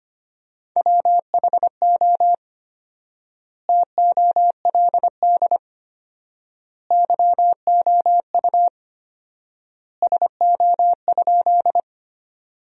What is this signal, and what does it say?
Morse code, heard as a keyed tone, about 25 wpm, 700 Hz. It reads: WHO TOLD YOU SO?
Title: 4x Word-Spacing - Sentences from Top 400 Words - 25wpm